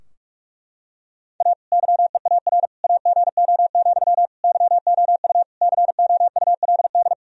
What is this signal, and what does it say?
Morse code, heard as a keyed tone, about 41 wpm, 700 Hz. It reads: A YEAR AGO, YOU COULD
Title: Sentences from Top 1000 Words - Review Part 3 - 45wpm